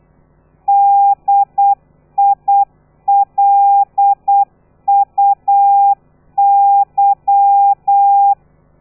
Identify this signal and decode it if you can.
Morse code, heard as a keyed tone, about 8 wpm, 800 Hz. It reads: DILUY
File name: telegrafia / i2tfj / Lez13 35